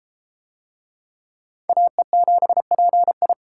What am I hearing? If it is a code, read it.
Morse code, heard as a keyed tone, about 30 wpm, 700 Hz. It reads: AE7PI